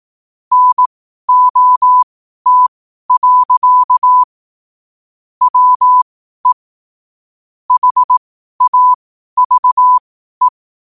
Morse code, heard as a keyed tone, about 18 wpm, spaced as for 12 wpm, 1000 Hz. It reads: NOT. WE HAVE
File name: qotdfiles / 12wpm